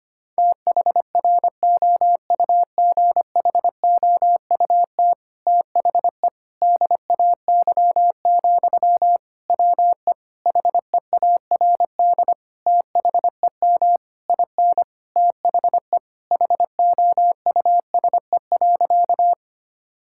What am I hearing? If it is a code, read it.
Morse code, heard as a keyed tone, about 25 wpm, 700 Hz. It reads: THROUGHOUT THE DAY, WE HEARD THEM IN THE HOUSE.
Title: Sentences from Top 1000 Words - Review Part 5 - 25wpm